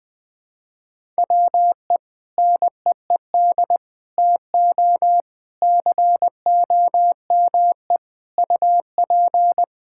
Morse code, heard as a keyed tone, about 19 wpm, 700 Hz. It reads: WE NEED TO COME UP
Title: Sentences with 7-Word Phrases - 20wpm